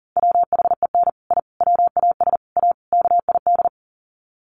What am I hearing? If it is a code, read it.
Morse code, heard as a keyed tone, about 40 wpm, 700 Hz. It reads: WHEN I WAS A KID
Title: Sets of 5 Words 40wpm